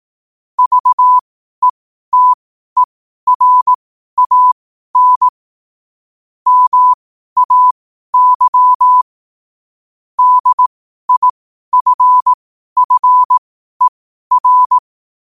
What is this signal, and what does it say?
Morse code, heard as a keyed tone, about 18 wpm, spaced as for 12 wpm, 1000 Hz. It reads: VETERAN MAY DIFFER
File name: qotdfiles / 12wpm